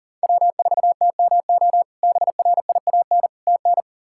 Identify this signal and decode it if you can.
Morse code, heard as a keyed tone, about 37 wpm, 700 Hz. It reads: W4TMO BRIAN TN
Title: K1USN SST QSO Practice - 40wpm